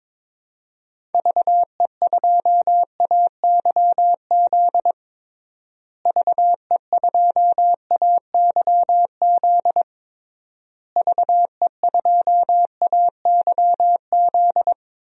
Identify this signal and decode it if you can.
Morse code, heard as a keyed tone, about 22 wpm, 700 Hz. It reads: VE2AYZ VE2AYZ VE2AYZ
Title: DXCC-100 Call Signs - 6 Characters - Repeated 3x - 22wpm